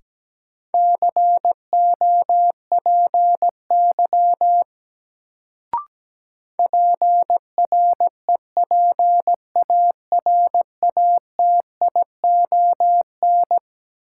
Morse code, heard as a keyed tone, about 17 wpm, 700 Hz. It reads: COPY PREPARATION
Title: Sentences from Top 1400 Words - New Word Drill Alt - 17wpm